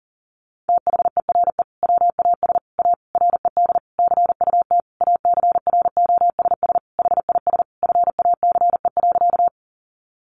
Morse code, heard as a keyed tone, about 40 wpm, 700 Hz. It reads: THERE WAS A RED CUT ACROSS HIS FACE.
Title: Sentences from Top 700 Words 40wpm